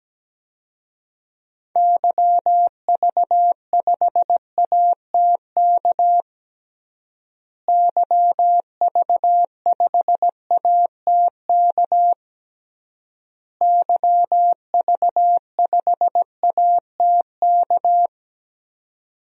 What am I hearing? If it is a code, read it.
Morse code, heard as a keyed tone, about 17 wpm, 700 Hz. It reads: YV5ATK YV5ATK YV5ATK
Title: DXCC-100 Call Signs - 6 Characters - Repeated 3x - 17wpm